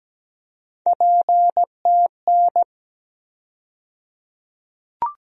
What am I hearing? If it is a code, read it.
Morse code, heard as a keyed tone, about 17 wpm, 700 Hz. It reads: PTN
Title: USA Call Sign Postfixes - 3 Letters 17wpm